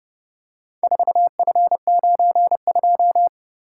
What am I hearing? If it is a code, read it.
Morse code, heard as a keyed tone, about 27 wpm, 700 Hz. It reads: 4F92